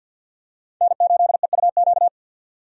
Morse code, heard as a keyed tone, about 49 wpm, 700 Hz. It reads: N8EUQ
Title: USA Call Signs - 5 Characters - 50wpm